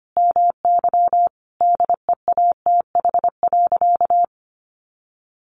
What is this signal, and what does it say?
Morse code, heard as a keyed tone, about 25 wpm, 700 Hz. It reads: MY DEATH.
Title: Sentences from Top 1500 Words 25wpm